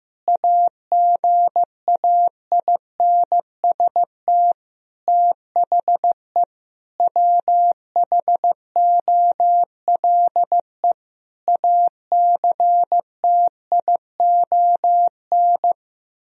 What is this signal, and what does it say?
Morse code, heard as a keyed tone, about 15 wpm, 700 Hz. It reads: AGAINST THE WHOLE ACTION
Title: Sentences from Top 1000 Words - Review Part 4 - 15wpm